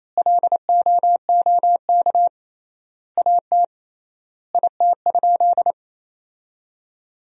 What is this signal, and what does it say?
Morse code, heard as a keyed tone, about 28 wpm, 700 Hz. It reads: LOOK AT IT?
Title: 3x Word-Spacing - Sentences from Top 100 Words - 28wpm